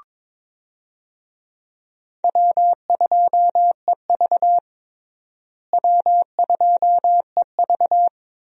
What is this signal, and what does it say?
Morse code, heard as a keyed tone, about 22 wpm, 700 Hz. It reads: W2EV W2EV
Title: USA Call Signs - 4 Characters - Repeated 2x - 22wpm